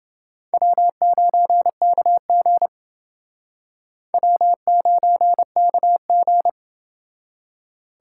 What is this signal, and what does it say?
Morse code, heard as a keyed tone, about 28 wpm, 700 Hz. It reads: W9KG W9KG